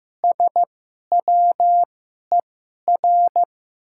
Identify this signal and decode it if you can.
Morse code, heard as a keyed tone, about 15 wpm, spaced as for 11 wpm, 700 Hz. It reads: SWER
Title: 2x Character Spacing - Top 300 Words - 15wpm